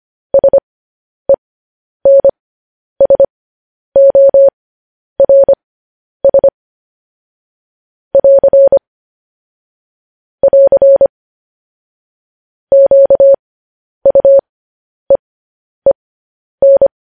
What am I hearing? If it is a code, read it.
Morse code, heard as a keyed tone, about 25 wpm, spaced as for 10 wpm, 550 Hz. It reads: SENSORS <AR> <AR> QUEEN
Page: news headlines in morse code at 10 wpm